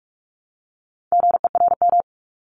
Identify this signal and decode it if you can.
Morse code, heard as a keyed tone, about 46 wpm, 700 Hz. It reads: GERM
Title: Multi-letter Phonemes Repeated 3x 45wpm